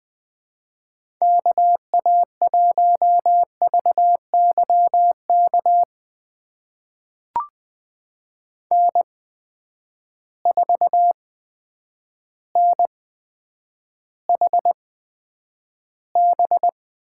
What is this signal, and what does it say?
Morse code, heard as a keyed tone, about 20 wpm, spaced as for 9 wpm, 700 Hz. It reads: KA1VYK N4NHB
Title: USA Call Signs - 8x Character Spacing - 20wpm